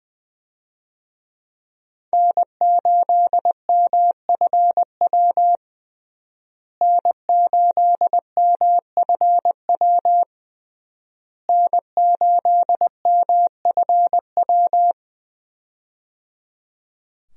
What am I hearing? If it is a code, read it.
Morse code, heard as a keyed tone, about 21 wpm, 700 Hz. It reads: N8MFW N8MFW N8MFW
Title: USA Call Signs - 5 Characters - Repeated 3x - 20wpm